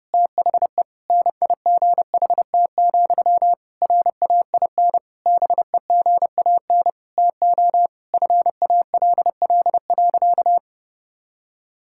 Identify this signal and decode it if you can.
Morse code, heard as a keyed tone, about 30 wpm, 700 Hz. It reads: THE NIGHT, RAIN BEGAN TO FALL.
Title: Sentences from Top 1000 Words - Review Part 6 - 30wpm